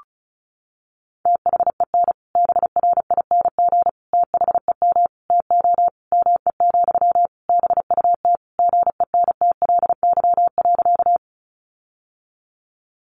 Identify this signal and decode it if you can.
Morse code, heard as a keyed tone, about 34 wpm, 700 Hz. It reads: THEN BRING THEM TO ME, BUT GENTLY.
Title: Sentences from Top 1500 Words 35wpm